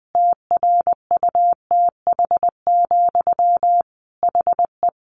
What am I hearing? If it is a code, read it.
Morse code, heard as a keyed tone, about 20 wpm, 700 Hz. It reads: TRUTH, HE
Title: Sentences from Top 800 Words 20wpm